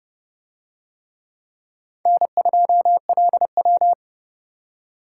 Morse code, heard as a keyed tone, about 30 wpm, 700 Hz. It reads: N2LW